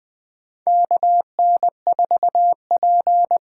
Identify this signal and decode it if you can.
Morse code, heard as a keyed tone, about 20 wpm, 700 Hz. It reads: KN4P
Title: USA Call Signs - 6 Characters - 8x Character Spacing - 20wpm